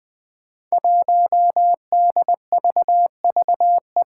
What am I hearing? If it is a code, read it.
Morse code, heard as a keyed tone, about 21 wpm, 700 Hz. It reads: 1DVVE
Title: Five Challenging Letter-Number Combinations 20wpm